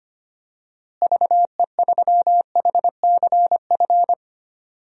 Morse code, heard as a keyed tone, about 25 wpm, 700 Hz. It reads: VE3HCF
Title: DXCC-100 Call Signs - 6 Characters - Repeated 3x - 25wpm